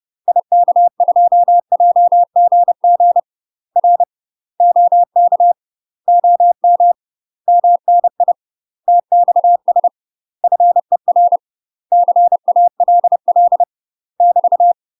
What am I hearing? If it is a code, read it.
Morse code, heard as a keyed tone, about 30 wpm, 700 Hz. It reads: IK2JGG R OK OM MNI TXS FER CALL <BT>
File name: UK MASTER QSOs TRAINING / 30WPM / UK MASTER QSOs 0027